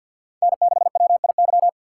Morse code, heard as a keyed tone, about 51 wpm, 700 Hz. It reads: N6WIY
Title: USA Call Signs - 5 Characters - 3x Character Spacing - 50wpm